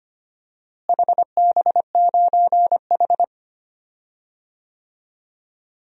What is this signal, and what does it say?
Morse code, heard as a keyed tone, about 25 wpm, 700 Hz. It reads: HB9H
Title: DXCC-100 Call Signs - 4 Characters - Repeated 3x - 25wpm